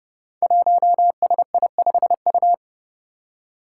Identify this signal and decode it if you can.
Morse code, heard as a keyed tone, about 30 wpm, 700 Hz. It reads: 1SI5U
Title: Five Challenging Letter-Number Combinations 30wpm